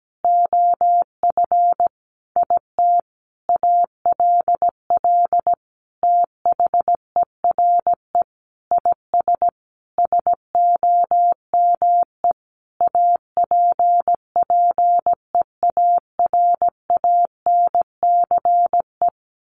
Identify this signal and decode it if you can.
Morse code, heard as a keyed tone, about 17 wpm, 700 Hz. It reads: OF IT ALL THERE IS SOME APPEARANCE
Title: Sentences from Top 1100 Words 17wpm